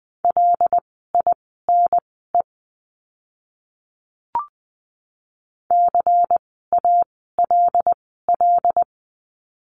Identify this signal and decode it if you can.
Morse code, heard as a keyed tone, about 20 wpm, spaced as for 15 wpm, 700 Hz. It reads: LINE CALL
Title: 115 Lesson 19f - Letter w - Words 20wpm